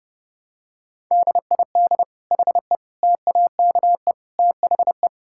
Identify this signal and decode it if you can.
Morse code, heard as a keyed tone, about 30 wpm, 700 Hz. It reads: DID HE TAKE THE
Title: Sentences from Top 1000 Words - Review Part 6 - 30wpm